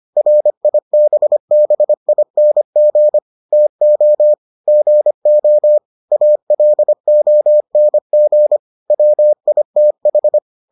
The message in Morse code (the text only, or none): RIBBING TO GO ALONG WITH